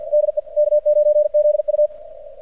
Morse code, heard as a keyed tone, about 53 wpm, 600 Hz, with radio noise. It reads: DE K9ZU
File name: archived / CRSnet